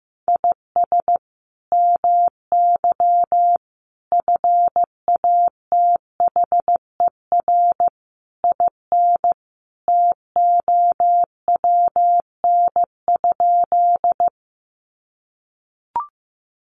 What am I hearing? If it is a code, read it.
Morse code, heard as a keyed tone, about 15 wpm, 700 Hz. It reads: IS MY FATHER IN TOWN?
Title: Sentences from Top 500 Words 15wpm